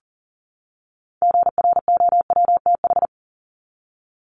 Morse code, heard as a keyed tone, about 40 wpm, 700 Hz. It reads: GROWTH
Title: Top 751-1000 Words 40wpm